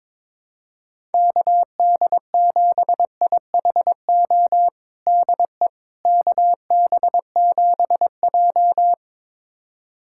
Morse code, heard as a keyed tone, about 22 wpm, 700 Hz. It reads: KD7IHO DE KB7J
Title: USA Callsign DE Callsign 22wpm